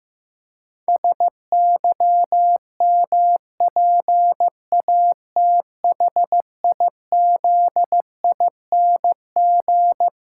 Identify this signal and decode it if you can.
Morse code, heard as a keyed tone, about 15 wpm, 700 Hz. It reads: SYMPATHIZING
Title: Pangram Challenge v2 - 15wpm